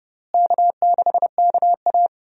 Morse code, heard as a keyed tone, about 30 wpm, 700 Hz. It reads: K6KA
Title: USA Call Signs - 4 Characters - Repeated 2x - 30wpm